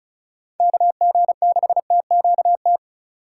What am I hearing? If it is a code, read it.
Morse code, heard as a keyed tone, about 47 wpm, spaced as for 41 wpm, 700 Hz. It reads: KG6TQT
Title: Speed Racing - USA 2x3 Call Signs 35wpm